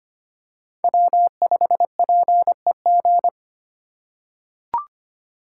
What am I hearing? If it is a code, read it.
Morse code, heard as a keyed tone, about 25 wpm, 700 Hz. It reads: W5PEG
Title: USA Call Signs - 3x Character Spacing - 25wpm